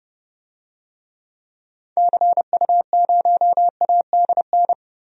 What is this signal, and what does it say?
Morse code, heard as a keyed tone, about 30 wpm, 700 Hz. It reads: CU0ADN
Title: DXCC-100 Call Signs - 6 Characters - Repeated 3x - 30wpm